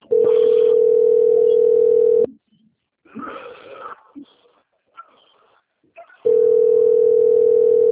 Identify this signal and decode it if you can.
Ringback tone (North America), 440 + 480 Hz.